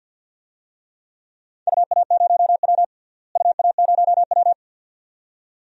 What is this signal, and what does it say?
Morse code, heard as a keyed tone, about 47 wpm, 700 Hz. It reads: UA0W UA0W
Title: DXCC-100 Call Signs - 4 Characters - Repeated 2x - 50wpm